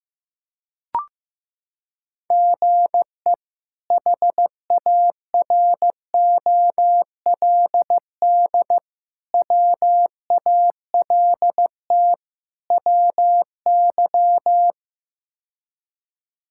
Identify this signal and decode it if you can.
Morse code, heard as a keyed tone, about 15 wpm, 700 Hz. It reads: GE HAROLD WALT WY
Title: K1USN SST QSO Practice - 15wpm